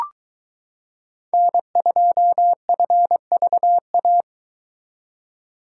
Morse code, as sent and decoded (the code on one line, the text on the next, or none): -. ..--- ..-. ...- .-
N2FVA